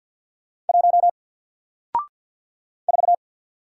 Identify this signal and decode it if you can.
Morse code, heard as a keyed tone, about 50 wpm, 700 Hz. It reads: 1 4
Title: Single Letter-Number Alt 50wpm